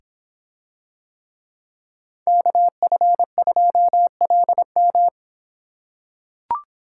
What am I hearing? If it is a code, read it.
Morse code, heard as a keyed tone, about 26 wpm, 700 Hz. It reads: KF2LM